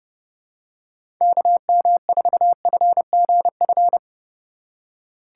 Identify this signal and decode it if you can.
Morse code, heard as a keyed tone, about 30 wpm, 700 Hz. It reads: KM4FGF
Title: USA Call Signs - 6 Characters - 2x Character Spacing - 30wpm